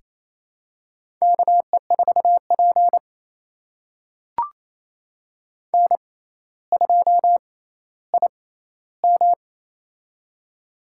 Morse code, heard as a keyed tone, about 27 wpm, spaced as for 14 wpm, 700 Hz. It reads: KE4P N2IM
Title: USA Call Signs - 4 Characters - 6x Character Spacing - 28wpm